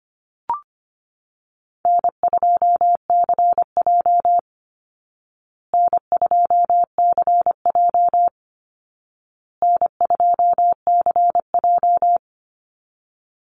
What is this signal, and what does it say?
Morse code, heard as a keyed tone, about 25 wpm, 700 Hz. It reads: N2CJ N2CJ N2CJ
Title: Easy Call Signs Repeated 3x - 25wpm